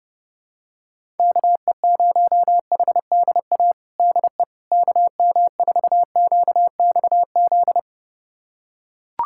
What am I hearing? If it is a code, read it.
Morse code, heard as a keyed tone, about 30 wpm, 700 Hz. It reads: KE0HDA DE KM4QXZ
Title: USA Callsign DE Callsign - 6 Characters - 30wpm